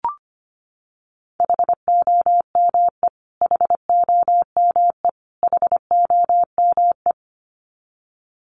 Morse code, heard as a keyed tone, about 25 wpm, 700 Hz. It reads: HOME HOME HOME
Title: Top 500 Words Repeated 25wpm